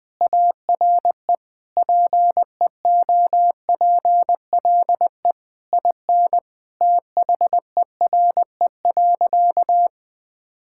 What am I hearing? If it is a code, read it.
Morse code, heard as a keyed tone, about 20 wpm, 700 Hz. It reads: ARE PEOPLE IN THERE.
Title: Sentences with 3-Word Phrases - Spoken Beginning - 20wpm